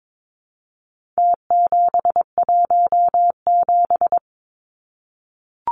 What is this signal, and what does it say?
Morse code, heard as a keyed tone, about 22 wpm, 700 Hz. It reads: T717
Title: Numbers - Contest Style - 4-Digit 22wpm